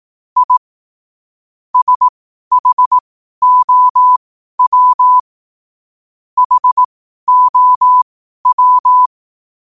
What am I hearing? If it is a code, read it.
Morse code, heard as a keyed tone, about 18 wpm, spaced as for 12 wpm, 1000 Hz. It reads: I SHOW HOW
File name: qotdfiles / 12wpm